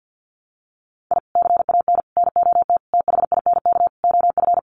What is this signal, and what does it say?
Morse code, heard as a keyed tone, about 50 wpm, 700 Hz. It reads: I CAN NOT THINK OF
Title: Sets of 5 Words 50wpm